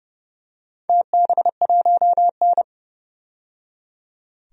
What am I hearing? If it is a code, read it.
Morse code, heard as a keyed tone, about 30 wpm, 700 Hz. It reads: TB1N
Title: DXCC-100 Call Signs - 4 Characters - 30wpm